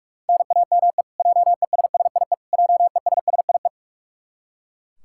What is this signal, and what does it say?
Morse code, heard as a keyed tone, about 42 wpm, 700 Hz. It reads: NAME JESSIE JESSIE
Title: Practice QSO - Exchange 1 - Beginning - 45wpm